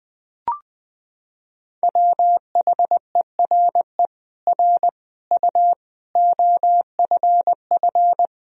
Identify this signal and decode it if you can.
Morse code, heard as a keyed tone, about 20 wpm, 700 Hz. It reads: WHERE R U OFF
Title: Sentences from Top 400 Words and Abbreviations 20wpm